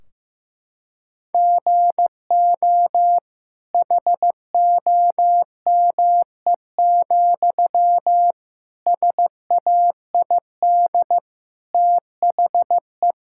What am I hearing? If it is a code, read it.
Morse code, heard as a keyed tone, about 15 wpm, 700 Hz. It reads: GO HOME, SAID THE
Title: Sentences from Top 1600 Words 15wpm